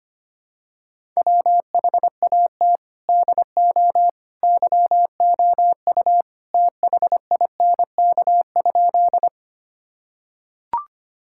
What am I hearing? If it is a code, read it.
Morse code, heard as a keyed tone, about 25 wpm, 700 Hz. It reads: WHAT DO YOU THINK?
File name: practice / 25wpm / Sentences with 3-Word Phrases - 25wpm